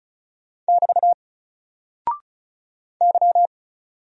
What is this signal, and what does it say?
Morse code, heard as a keyed tone, about 36 wpm, 700 Hz. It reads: <BT> Y
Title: Single Character (Letter:Number:Punctuation:Prosign) 35wpm